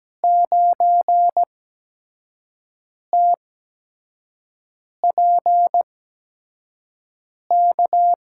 Morse code, heard as a keyed tone, about 17 wpm, spaced as for 5 wpm, 700 Hz. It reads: 9TPK
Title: USA Call Signs - 5 Characters - 8x Character Spacing - 17wpm